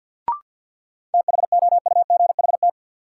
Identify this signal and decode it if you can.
Morse code, heard as a keyed tone, about 50 wpm, 700 Hz. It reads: THOUGHT
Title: Sentences from Top 300 Words - New Word Drill - 50wpm